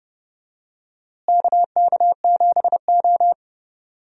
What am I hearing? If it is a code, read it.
Morse code, heard as a keyed tone, about 30 wpm, 700 Hz. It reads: KK7O
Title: USA Call Signs - 4 Characters - Repeated 3x - 30wpm